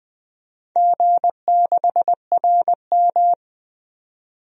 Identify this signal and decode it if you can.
Morse code, heard as a keyed tone, about 20 wpm, 700 Hz. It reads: G6RM
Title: DXCC-100 Call Signs - 4 Characters - 20wpm